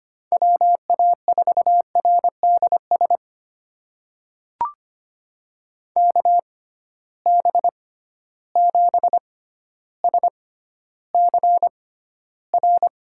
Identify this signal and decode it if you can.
Morse code, heard as a keyed tone, about 25 wpm, spaced as for 13 wpm, 700 Hz. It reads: WA4RDS KB7SCR
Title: USA Call Signs - 6 Characters - 6x Character Spacing - 25wpm